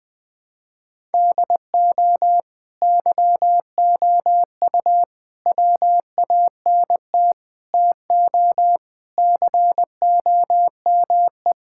Morse code, heard as a keyed tone, about 20 wpm, 700 Hz. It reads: DO YOU WANT TO COME
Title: Sentences with 3-Word Phrases - 20wpm